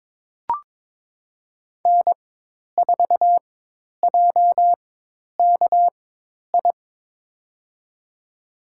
Morse code, heard as a keyed tone, about 22 wpm, spaced as for 10 wpm, 700 Hz. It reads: N4JKI
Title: USA Call Signs - 5 Characters - 4x Character Spacing - 22wpm